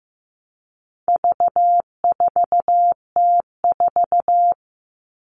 Morse code, heard as a keyed tone, about 15 wpm, 700 Hz. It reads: V4T4
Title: All Cut Numbers - 4-Digit 15wpm